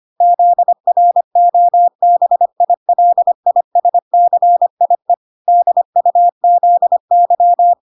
Morse code, heard as a keyed tone, about 25 wpm, 700 Hz. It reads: ZROBILISCIE DUZY